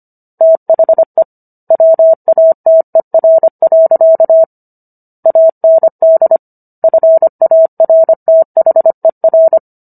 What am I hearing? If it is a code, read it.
Morse code, heard as a keyed tone, about 25 wpm, 650 Hz. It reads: THE WATER. AND FARTHER